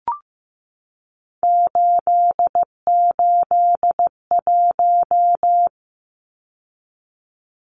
Morse code, heard as a keyed tone, about 15 wpm, 700 Hz. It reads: 881
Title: Numbers - 3-Digit 15wpm